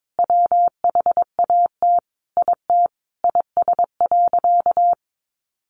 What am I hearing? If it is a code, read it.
Morse code, heard as a keyed tone, about 22 wpm, 700 Hz. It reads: WHAT IT IS.
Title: Sentences from Top 200 Words 22wpm